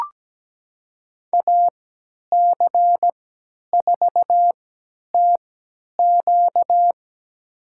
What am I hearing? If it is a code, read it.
Morse code, heard as a keyed tone, about 17 wpm, spaced as for 10 wpm, 700 Hz. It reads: AC4TQ
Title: USA Call Signs - 5 Characters - 3x Character Spacing - 17wpm